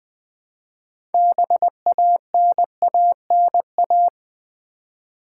Morse code, heard as a keyed tone, about 20 wpm, 690 Hz. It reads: BANANA
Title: Easy Words with Hints - 20wpm